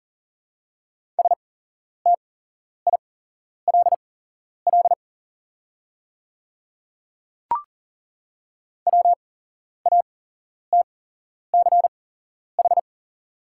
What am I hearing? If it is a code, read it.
Morse code, heard as a keyed tone, about 39 wpm, spaced as for 11 wpm, 700 Hz. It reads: STILL WATCH
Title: 8x Character Spacing - Top 300 Words - 40wpm